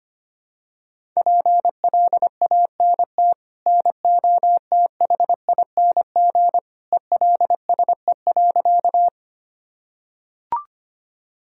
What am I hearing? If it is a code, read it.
Morse code, heard as a keyed tone, about 25 wpm, 700 Hz. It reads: PLANT NOTHING ELSE.
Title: Sentences from Top 1000 Words - Review Part 1 - 25wpm